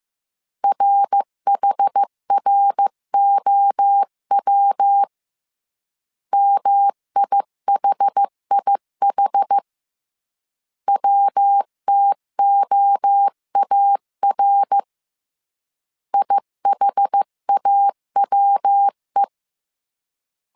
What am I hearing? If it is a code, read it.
Morse code, heard as a keyed tone, about 15 wpm, 800 Hz. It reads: RHROW MIHIH WTOAR IHAWE
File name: CW / DET / 1 3 1500C S C 14wpm 2458